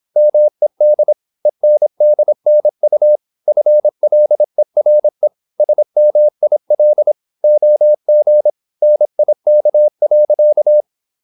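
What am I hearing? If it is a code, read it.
Morse code, heard as a keyed tone, about 26 wpm, 600 Hz. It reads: MED ENDNU FLERE SMIL OG NIK.
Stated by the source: High-speed (QRQ) Morse at 26 wpm.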